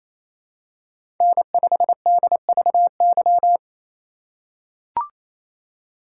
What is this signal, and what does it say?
Morse code, heard as a keyed tone, about 28 wpm, 700 Hz. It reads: N5DVY
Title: USA Call Signs - 5 Characters - 4x Character Spacing - 28wpm